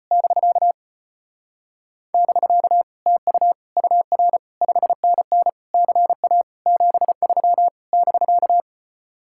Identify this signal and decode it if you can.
Morse code, heard as a keyed tone, about 33 wpm, 700 Hz. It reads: <BK> <BK> TU UR 5NN CA 73 <BK>